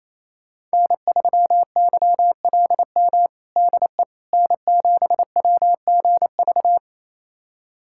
Morse code, heard as a keyed tone, about 27 wpm, 700 Hz. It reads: N3YLM DE N7WGV
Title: USA Callsign DE Callsign - 5 Characters - 28wpm